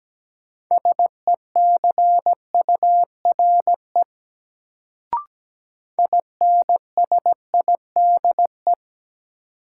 Morse code, heard as a keyed tone, about 17 wpm, 700 Hz. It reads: SECURE INSIDE
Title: Sentences from Top 1200 Words - New Word Drill Alt - 17wpm